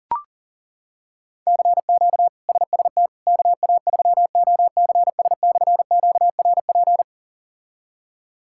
Morse code, heard as a keyed tone, about 40 wpm, 700 Hz. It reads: CQ SST KA3OCS/QRP
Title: K1USN SST QSO Practice - 40wpm